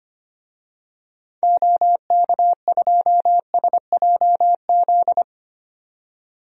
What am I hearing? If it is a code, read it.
Morse code, heard as a keyed tone, about 25 wpm, 690 Hz. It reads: OK2SJZ
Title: DXCC-100 Call Signs - 6 Characters - 25wpm